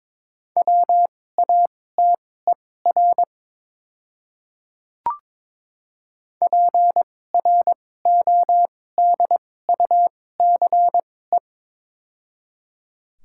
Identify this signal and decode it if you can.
Morse code, heard as a keyed tone, about 22 wpm, spaced as for 16 wpm, 700 Hz. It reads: WATER PRODUCE
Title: 2x Character Spacing - Top 500 Words - 22wpm